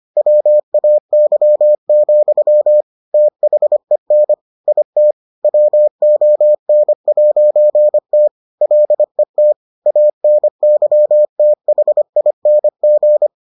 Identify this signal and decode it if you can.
Morse code, heard as a keyed tone, about 25 wpm, 600 Hz. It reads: WAY, THEN IT WON'T LET ANYTHING